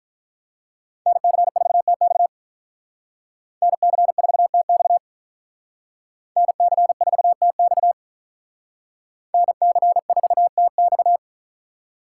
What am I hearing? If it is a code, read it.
Morse code, heard as a keyed tone, about 46 wpm, 700 Hz. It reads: NC4TX NC4TX NC4TX NC4TX